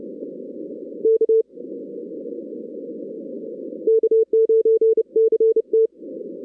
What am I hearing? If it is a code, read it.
Morse code, heard as a keyed tone, about 31 wpm, 440 Hz. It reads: K K9CT
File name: 2016 arrl dx cw / 2016-ARRL-DX-CW-K5TR- - 0-22